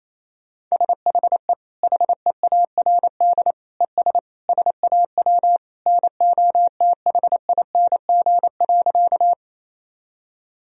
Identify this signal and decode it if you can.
Morse code, heard as a keyed tone, about 28 wpm, 700 Hz. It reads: SHE HEARD ES SAW NOTHING.
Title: Sentences from Top 500 Words and Abbreviations 28wpm